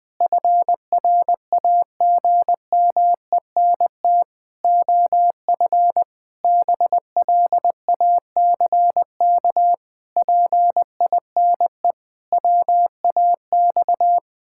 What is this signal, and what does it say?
Morse code, heard as a keyed tone, about 20 wpm, 700 Hz. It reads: FRAGMENT OF BLACK PINE WAX
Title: Pangram Challenge v2 - 20wpm